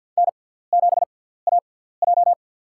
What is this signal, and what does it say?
Morse code, heard as a keyed tone, about 52 wpm, spaced as for 17 wpm, 700 Hz. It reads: N7AJ
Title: USA Call Signs - 6x Character Spacing - 50wpm